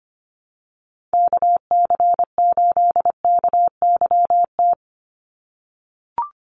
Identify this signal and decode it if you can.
Morse code, heard as a keyed tone, about 25 wpm, 700 Hz. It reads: KC8KYT
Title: USA Call Signs 25wpm